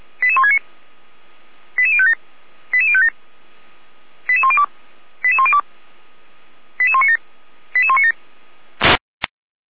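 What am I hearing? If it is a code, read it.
Five-tone selective call (ZVEI), heard as quick runs of five tones, tones of about 70 ms. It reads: 88169 88057 88057 88102 88102 88197 88197
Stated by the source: ZVEI selective call 88168 received on an SDR.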